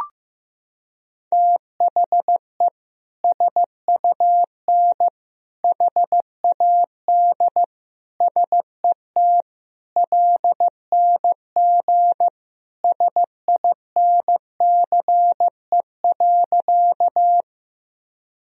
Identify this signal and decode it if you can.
Morse code, heard as a keyed tone, about 15 wpm, 700 Hz. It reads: THE SUN HAD SET LNG SINCE.
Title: Sentences from Top 300 Words and Abbreviations 15wpm